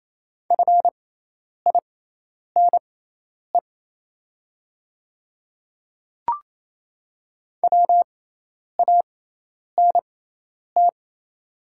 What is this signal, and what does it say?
Morse code, heard as a keyed tone, about 28 wpm, spaced as for 10 wpm, 700 Hz. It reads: FINE WANT
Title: 6x Character Spacing - Top 500 Words - 28wpm